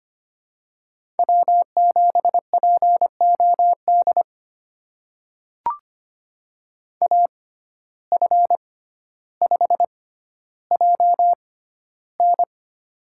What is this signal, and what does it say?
Morse code, heard as a keyed tone, about 25 wpm, spaced as for 13 wpm, 700 Hz. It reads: W7POD AF5JN
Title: USA Call Signs - 5 Characters - 6x Character Spacing - 25wpm